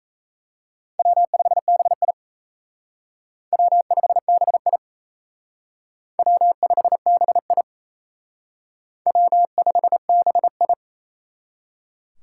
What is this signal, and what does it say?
Morse code, heard as a keyed tone, about 36 wpm, 700 Hz. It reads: W5BI W5BI W5BI W5BI